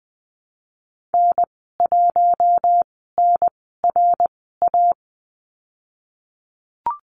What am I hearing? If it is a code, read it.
Morse code, heard as a keyed tone, about 20 wpm, 700 Hz. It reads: N 1 N R A
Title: 082 Lesson 14g - Number 2 - Call Signs 20wpm